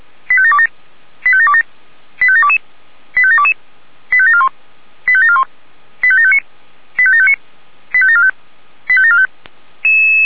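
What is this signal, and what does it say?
Five-tone selective call (ZVEI), heard as quick runs of five tones, tones of about 70 ms. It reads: 85727 85727 85722 85722 85741 85741 85759 85759 85745 85745
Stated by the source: ZVEI five-tone selective call received by SDR.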